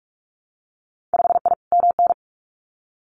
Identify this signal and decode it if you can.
Morse code, heard as a keyed tone, about 46 wpm, 700 Hz. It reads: 5I MN
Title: Winter Field Day Exchanges Encore 45wpm